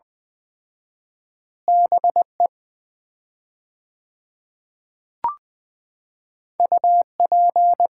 Morse code, heard as a keyed tone, about 21 wpm, 700 Hz. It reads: BE UP
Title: 4x Word-Spacing - Sets of 2 Words - 20wpm